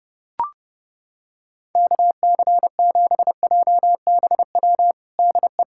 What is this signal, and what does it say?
Morse code, heard as a keyed tone, about 30 wpm, 690 Hz. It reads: KC7JBW DE
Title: USA Callsign DE Callsign 30wpm